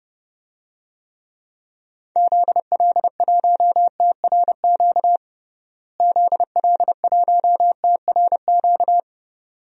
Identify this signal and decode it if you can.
Morse code, heard as a keyed tone, about 30 wpm, 700 Hz. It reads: ZL1TRQ ZL1TRQ
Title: DXCC-100 Call Signs - 6 Characters - Repeated 2x - 30wpm